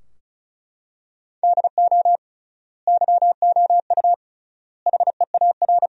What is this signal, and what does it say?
Morse code, heard as a keyed tone, about 36 wpm, 700 Hz. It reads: DO YOU HEAR
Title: 3x Word-Spacing - Sentences from Top 300 Words - 35wpm